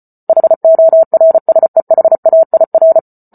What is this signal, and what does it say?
Morse code, heard as a keyed tone, about 34 wpm, 650 Hz. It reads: HORSEHAIR